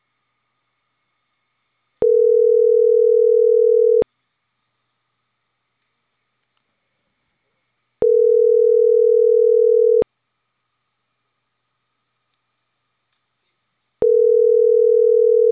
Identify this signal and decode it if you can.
Ringback tone (North America), 440 + 480 Hz.
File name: CENTRAL CAEA / out agcaea03